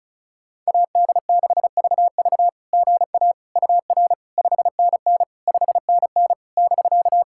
Morse code, heard as a keyed tone, about 34 wpm, 690 Hz. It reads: AD6VV GA UR 5NN 5NN <BK>
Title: SOTA QSO Practice 35wpm